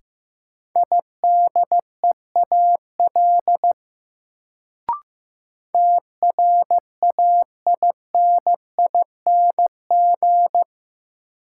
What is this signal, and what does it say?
Morse code, heard as a keyed tone, about 15 wpm, 700 Hz. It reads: IDEAL TRAINING
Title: Sentences from Top 1400 Words - New Word Drill Alt - 15wpm